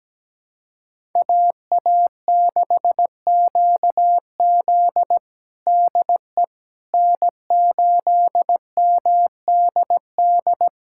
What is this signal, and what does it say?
Morse code, heard as a keyed tone, about 17 wpm, 700 Hz. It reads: AA6QZ DE N8MDD
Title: USA Callsign DE Callsign - 5 Characters - 17wpm